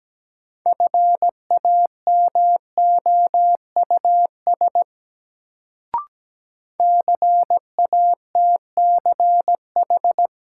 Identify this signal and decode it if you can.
Morse code, heard as a keyed tone, about 17 wpm, 700 Hz. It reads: FAMOUS CATCH
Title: Sentences from Top 1500 Words - New Word Drill Alt - 17wpm